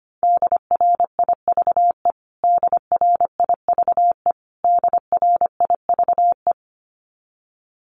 Morse code, heard as a keyed tone, about 25 wpm, 700 Hz. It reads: DRIVE DRIVE DRIVE
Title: Top 500 Words Repeated 25wpm